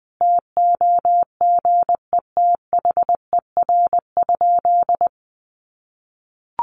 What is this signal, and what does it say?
Morse code, heard as a keyed tone, about 19 wpm, 700 Hz. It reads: TOGETHER?
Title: Sentences from Top 1400 Words 20wpm